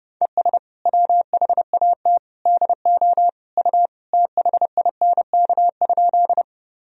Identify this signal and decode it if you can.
Morse code, heard as a keyed tone, about 30 wpm, 700 Hz. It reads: ES WHAT DO U THINK?
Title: Sentences from Top 200 Words and Abbreviations 30wpm